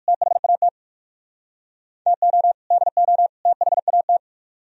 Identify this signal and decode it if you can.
Morse code, heard as a keyed tone, about 57 wpm, 700 Hz. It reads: THAT TO DO THAT